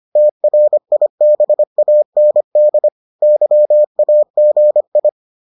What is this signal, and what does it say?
Morse code, heard as a keyed tone, about 25 wpm, 600 Hz. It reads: TRIBAND YAGI